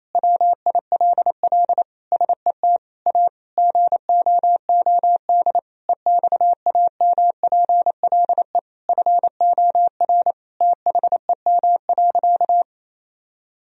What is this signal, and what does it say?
Morse code, heard as a keyed tone, about 28 wpm, 700 Hz. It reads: WILL SET A GOOD EXAMPLE FOR THEM.
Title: Sentences from Top 1000 Words - Review Part 1 - 28wpm